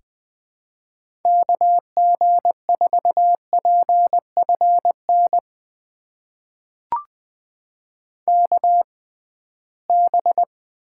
Morse code, heard as a keyed tone, about 20 wpm, spaced as for 15 wpm, 700 Hz. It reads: KG4PFN KB
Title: USA Call Signs - 6 Characters - 6x Character Spacing - 20wpm